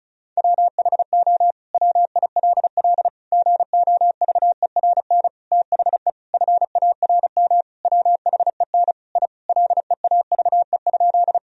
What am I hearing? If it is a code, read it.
Morse code, heard as a keyed tone, about 34 wpm, 700 Hz. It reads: WHO WILL GOVERN THE FARM WHEN I LEAVE?
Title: Sentences from Top 1000 Words - Review Part 2 - 35wpm